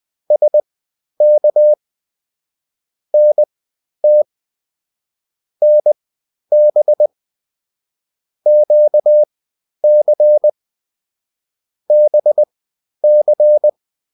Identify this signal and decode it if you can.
Morse code, heard as a keyed tone, about 20 wpm, spaced as for 11 wpm, 600 Hz. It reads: SK NT NB QC BC